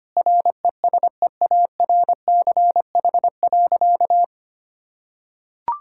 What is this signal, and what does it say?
Morse code, heard as a keyed tone, about 25 wpm, 700 Hz. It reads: RESEARCH.
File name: practice / 25wpm / Sentences with 9-Word Phrases - 25wpm